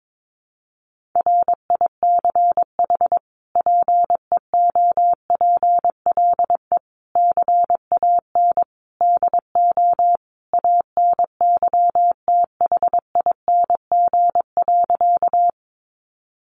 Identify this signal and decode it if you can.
Morse code, heard as a keyed tone, about 22 wpm, 700 Hz. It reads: RICH PEOPLE CAN DO ANYTHING.
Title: Sentences from Top 1100 Words 22wpm